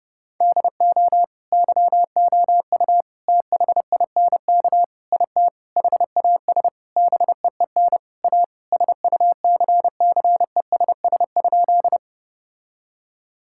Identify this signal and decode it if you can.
Morse code, heard as a keyed tone, about 32 wpm, 700 Hz. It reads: DO YOU THINK IT HAS BEEN A SUCCESS?
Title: Sentences from Top 1000 Words - Review Part 6 - 30wpm